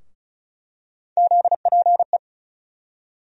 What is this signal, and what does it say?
Morse code, heard as a keyed tone, about 36 wpm, 700 Hz. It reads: ZPE
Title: USA Call Sign Postfixes - 3 Letters 35wpm